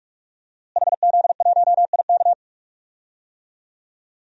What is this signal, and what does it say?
Morse code, heard as a keyed tone, about 46 wpm, 700 Hz. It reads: HZ1IK
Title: DXCC-100 Call Signs - 5 Characters - Repeated 3x - 45wpm